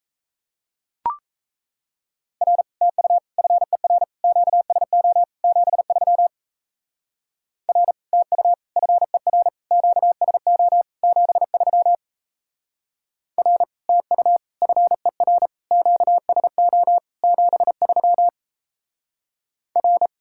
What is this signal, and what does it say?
Morse code, heard as a keyed tone, about 33 wpm, 700 Hz. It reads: R TU FER QSO 73 R TU FER QSO 73 R TU FER QSO 73 R
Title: Speed-Racing Common QSO Phrases Encore - 28wpm